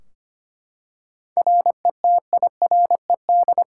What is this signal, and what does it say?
Morse code, heard as a keyed tone, about 25 wpm, 700 Hz. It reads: RETIRED
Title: K1USN SST QSO Practice 25wpm